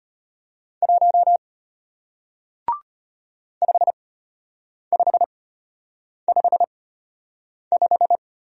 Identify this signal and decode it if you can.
Morse code, heard as a keyed tone, about 33 wpm, spaced as for 8 wpm, 700 Hz. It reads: 1 5555